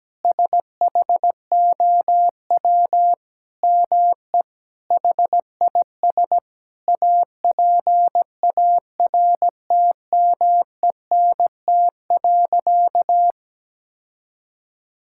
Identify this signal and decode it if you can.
Morse code, heard as a keyed tone, about 17 wpm, 700 Hz. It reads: SHOW ME HIS APARTMENT.
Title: Sentences from Top 1000 Words - Review Part 6 - 17wpm